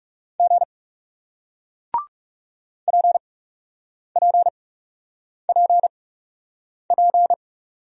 Morse code, heard as a keyed tone, about 38 wpm, spaced as for 8 wpm, 700 Hz. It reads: G PPPP